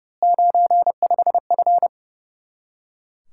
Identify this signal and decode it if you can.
Morse code, heard as a keyed tone, about 28 wpm, 700 Hz. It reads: 95F
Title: Speed-Racing Common QSO Phrases Encore - 30wpm